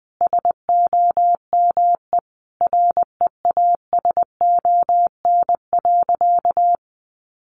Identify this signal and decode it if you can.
Morse code, heard as a keyed tone, about 20 wpm, 700 Hz. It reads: SOME REASON.
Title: Sentences from Top 700 Words 20wpm